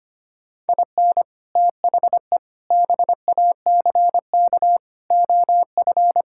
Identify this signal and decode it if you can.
Morse code, heard as a keyed tone, about 25 wpm, 700 Hz. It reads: IN THE BACK OF
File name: practice / 25wpm / Sentences with 4-Word Phrases - 25wpm